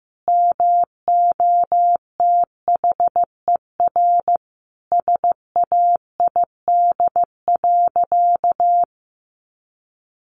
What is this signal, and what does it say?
Morse code, heard as a keyed tone, about 15 wpm, 700 Hz. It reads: MOTHER SAID.
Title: Sentences from Top 1400 Words 15wpm